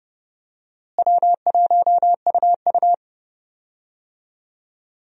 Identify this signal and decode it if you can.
Morse code, heard as a keyed tone, about 32 wpm, 700 Hz. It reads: W1UU
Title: USA Call Signs - 4 Characters - Repeated 3x - 30wpm